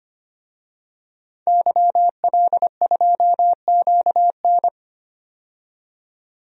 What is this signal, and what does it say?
Morse code, heard as a keyed tone, about 25 wpm, 700 Hz. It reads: YL2QN
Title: DXCC-100 Call Signs - 5 Characters - 25wpm